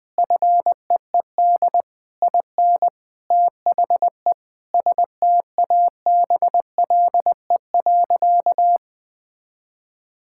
Morse code, heard as a keyed tone, about 20 wpm, 700 Hz. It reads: FEED IN THE STABLE.
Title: Sentences from Top 1600 Words 20wpm